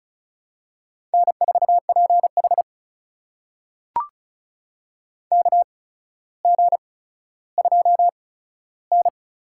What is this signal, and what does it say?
Morse code, heard as a keyed tone, about 36 wpm, spaced as for 15 wpm, 700 Hz. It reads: N4PH KG2N
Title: USA Call Signs - 4 Characters - 8x Character Spacing - 35wpm